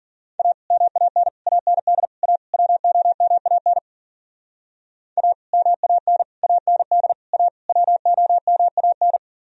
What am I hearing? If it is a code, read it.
Morse code, heard as a keyed tone, about 52 wpm, 700 Hz. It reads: A MAN AND A WOMAN A MAN AND A WOMAN